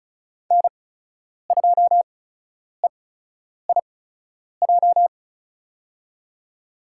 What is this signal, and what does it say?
Morse code, heard as a keyed tone, about 36 wpm, spaced as for 10 wpm, 700 Hz. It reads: N2EIJ
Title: USA Call Signs - 5 Characters - 8x Character Spacing - 35wpm